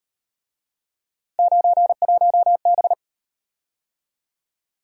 Morse code, heard as a keyed tone, about 38 wpm, 700 Hz. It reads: 91B